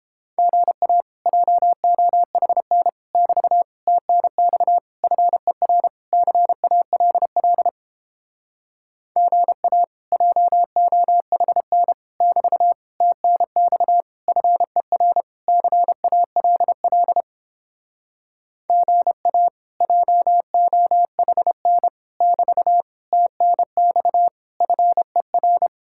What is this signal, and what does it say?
Morse code, heard as a keyed tone, about 29 wpm, 700 Hz. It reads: GA JOHN <BT> TNX FER CALL GA JOHN <BT> TNX FER CALL GA JOHN <BT> TNX FER